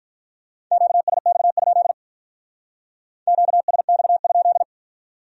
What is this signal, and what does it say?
Morse code, heard as a keyed tone, about 42 wpm, 700 Hz. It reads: QSX? QSX?
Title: Speed-Racing Q-Codes - 35wpm